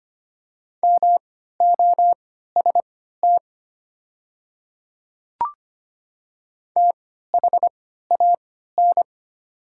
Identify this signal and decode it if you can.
Morse code, heard as a keyed tone, about 25 wpm, spaced as for 14 wpm, 700 Hz. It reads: MOST THAN
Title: 3x Character Spacing - Top 100 Words - 25wpm